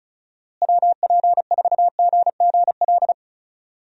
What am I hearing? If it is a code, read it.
Morse code, heard as a keyed tone, about 36 wpm, 700 Hz. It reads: WP4GGL
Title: USA Call Signs - 6 Characters - Repeated 3x - 35wpm